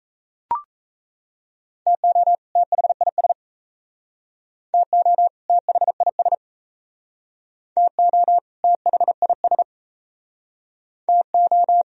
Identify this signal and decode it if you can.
Morse code, heard as a keyed tone, about 35 wpm, 700 Hz. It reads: TO THIS TO THIS TO THIS TO